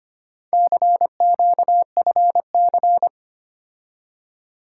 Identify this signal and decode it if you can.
Morse code, heard as a keyed tone, about 25 wpm, 700 Hz. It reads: CQFC
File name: practice / 25wpm / Four Letter-Number Combinations - 25wpm